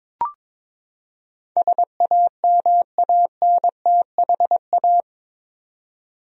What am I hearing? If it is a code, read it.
Morse code, heard as a keyed tone, about 21 wpm, 700 Hz. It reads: SAMANTHA
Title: USA Common Names - 22wpm